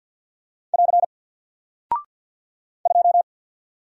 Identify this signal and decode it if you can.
Morse code, heard as a keyed tone, about 48 wpm, 700 Hz. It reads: <AR> 2
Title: Single Character (Letter:Number:Punctuation:Prosign) Alt 50wpm